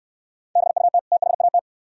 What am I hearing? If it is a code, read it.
Morse code, heard as a keyed tone, about 64 wpm, 690 Hz. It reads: BUT THAT